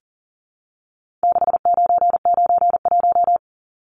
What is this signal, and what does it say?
Morse code, heard as a keyed tone, about 40 wpm, 700 Hz. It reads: 6991
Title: Numbers - 4-Digit 40wpm